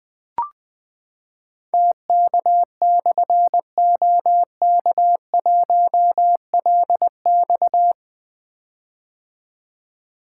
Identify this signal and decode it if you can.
Morse code, heard as a keyed tone, about 20 wpm, 700 Hz. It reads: TK/OK1LX
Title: Difficult Call Signs 20wpm